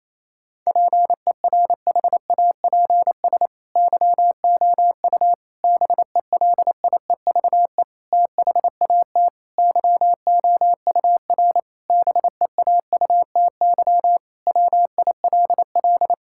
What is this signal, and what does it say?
Morse code, heard as a keyed tone, about 28 wpm, 700 Hz. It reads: PERHAPS YOU BELIEVE THAT YOUR BEAUTY WILL
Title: Sentences from Top 1600 Words 28wpm